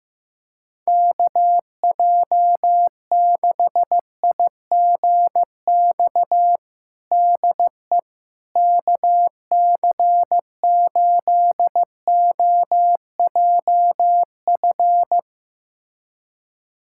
Morse code, heard as a keyed tone, about 15 wpm, 700 Hz. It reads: KJ6IGX DE KC8OJF
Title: USA Callsign DE Callsign - 6 Characters - 15wpm